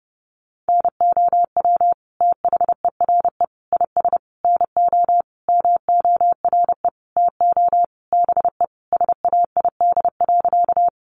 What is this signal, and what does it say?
Morse code, heard as a keyed tone, about 30 wpm, 700 Hz. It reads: NOW THERE IS NO MORE TO BE SAID.
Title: Sentences from Top 100 Words 30wpm